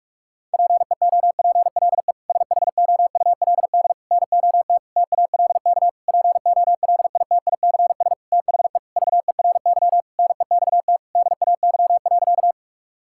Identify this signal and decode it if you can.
Morse code, heard as a keyed tone, about 41 wpm, 700 Hz. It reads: PEOPLE SHOULD NOT TALK POLITICS THE VERY NEXT DAY.
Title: Sentences from Top 1000 Words - Review Part 6 - 45wpm